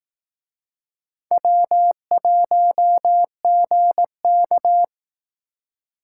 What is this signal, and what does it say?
Morse code, heard as a keyed tone, about 18 wpm, 700 Hz. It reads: W1GK
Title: Speed Racing - USA 1x2 Call Signs 15wpm